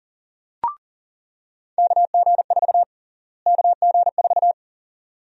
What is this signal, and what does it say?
Morse code, heard as a keyed tone, about 40 wpm, 700 Hz. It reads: KG4 KG4
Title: USA Call Sign Prefixes - Repeated 2x - 40wpm